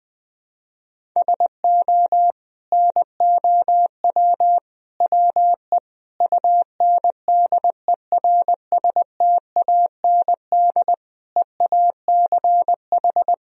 Morse code, heard as a keyed tone, about 21 wpm, 700 Hz. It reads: SO NOW WE UNDERSTAND EACH
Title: Sentences from Top 1000 Words - Review Part 1 - 20wpm